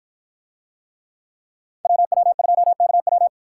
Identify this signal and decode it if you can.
Morse code, heard as a keyed tone, about 53 wpm, 700 Hz. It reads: WW2KW